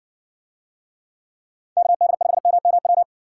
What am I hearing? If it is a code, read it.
Morse code, heard as a keyed tone, about 60 wpm, 700 Hz. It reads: KD5MMW